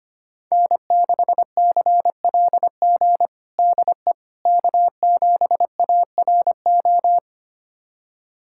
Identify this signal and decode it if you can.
Morse code, heard as a keyed tone, about 25 wpm, 700 Hz. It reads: N6CLG DE K7ARO
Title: USA Callsign DE Callsign - 5 Characters - 25wpm